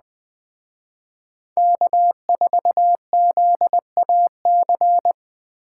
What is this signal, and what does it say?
Morse code, heard as a keyed tone, about 20 wpm, 700 Hz. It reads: K4ZAC